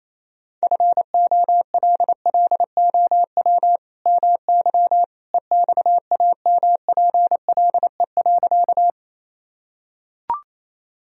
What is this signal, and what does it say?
Morse code, heard as a keyed tone, about 27 wpm, 700 Hz. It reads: FOLLOW MY EXAMPLE.
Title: Sentences from Top 1000 Words - Review Part 1 - 28wpm